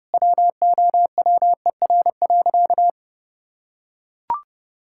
Morse code, heard as a keyed tone, about 30 wpm, 700 Hz. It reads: WOWER.
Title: Pangram Challenge v2 - 30wpm